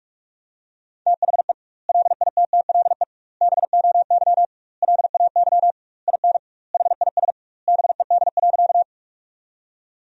Morse code, heard as a keyed tone, about 44 wpm, 700 Hz. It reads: THE LITTLE BOY LAY IN HIS BED.
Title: 2x Word-Spacing - Sentences from Top 500 Words - 45wpm